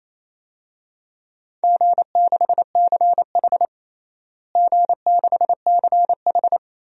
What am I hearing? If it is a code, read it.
Morse code, heard as a keyed tone, about 28 wpm, 700 Hz. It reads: G6CH G6CH
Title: DXCC-100 Call Signs - 4 Characters - Repeated 2x - 28wpm